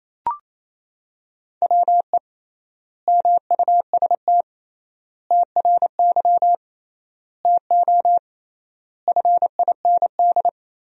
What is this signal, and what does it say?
Morse code, heard as a keyed tone, about 28 wpm, 700 Hz. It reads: WE MUST TRY TO FIND
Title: 3x Word-Spacing - Sentences from Top 400 Words - 28wpm